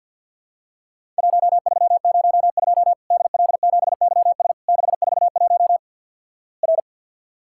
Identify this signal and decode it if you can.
Morse code, heard as a keyed tone, about 50 wpm, 700 Hz. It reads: 1302 DL7YS 641 R
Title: Worked All Europe DX Contest - QTC Exchange - 50wpm